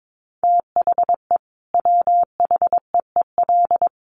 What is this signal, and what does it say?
Morse code, heard as a keyed tone, about 22 wpm, 700 Hz. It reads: THE WHEEL
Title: Additional Easy Phrases 22wpm